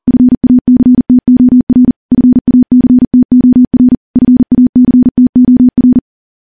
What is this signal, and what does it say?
Morse code, heard as a keyed tone, about 43 wpm, 250 Hz. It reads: FACTOR FACTOR FACTOR
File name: morse-training-40WPM